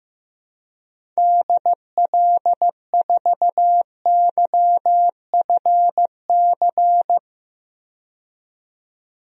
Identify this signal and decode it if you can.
Morse code, heard as a keyed tone, about 15 wpm, 700 Hz. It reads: DL4YFC
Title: DXCC-100 Call Signs - 6 Characters - 15wpm